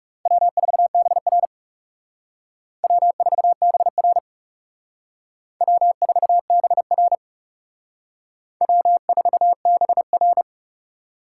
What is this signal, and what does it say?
Morse code, heard as a keyed tone, about 36 wpm, 700 Hz. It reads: W4BR W4BR W4BR W4BR
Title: Speed Racing - USA 1x2 Call Signs 30wpm